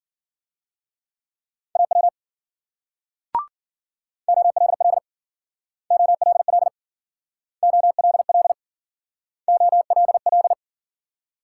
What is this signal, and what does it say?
Morse code, heard as a keyed tone, about 51 wpm, 700 Hz. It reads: AW OLL OLL OLL OLL